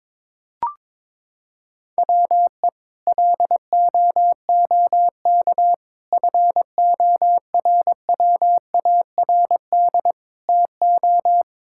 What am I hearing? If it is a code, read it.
Morse code, heard as a keyed tone, about 22 wpm, 700 Hz. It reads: WE LOOK FORWARD TO
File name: practice / 22wpm / Sets of 6 Words - 22wpm